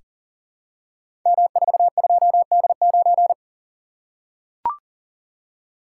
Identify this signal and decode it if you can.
Morse code, heard as a keyed tone, about 42 wpm, 690 Hz. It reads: M42D9
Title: Code Groups - 5 Letter:Numbers - 40wpm